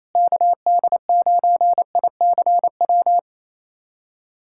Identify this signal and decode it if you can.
Morse code, heard as a keyed tone, about 28 wpm, 700 Hz. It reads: KD9ICW
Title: USA Callsign DE Callsign 28wpm